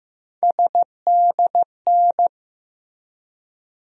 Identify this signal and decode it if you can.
Morse code, heard as a keyed tone, about 15 wpm, 700 Hz. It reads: SDN
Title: USA Call Sign Postfixes - 3 Letters 15wpm